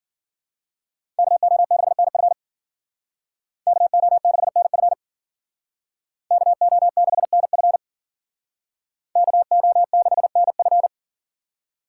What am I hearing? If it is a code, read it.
Morse code, heard as a keyed tone, about 48 wpm, 700 Hz. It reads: KO6NF KO6NF KO6NF KO6NF
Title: Speed Racing - USA 2x2 Call Signs 40wpm